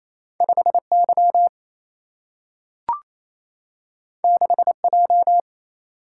Morse code, heard as a keyed tone, about 28 wpm, 700 Hz. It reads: 5Y 6J
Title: Single-Letter Single-Number 28wpm